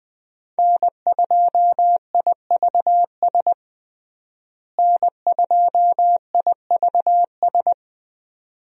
Morse code, heard as a keyed tone, about 21 wpm, 700 Hz. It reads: N2IVS N2IVS
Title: USA Call Signs - 5 Characters - Repeated 2x - 20wpm